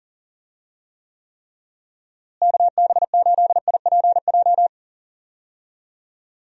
Morse code, heard as a keyed tone, about 40 wpm, 700 Hz. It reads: KB8IPJ
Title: USA Call Signs - 6 Characters - Repeated 3x - 40wpm